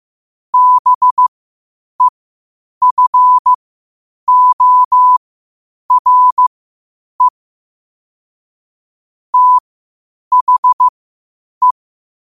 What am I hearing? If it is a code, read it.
Morse code, heard as a keyed tone, about 15 wpm, spaced as for 8 wpm, 1000 Hz. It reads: BEFORE THE
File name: qotdfiles / 8wpm